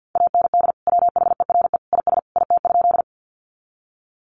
Morse code, heard as a keyed tone, about 48 wpm, 700 Hz. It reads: AND WHERE IS IT?
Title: Sentences from Top 200 Words 50wpm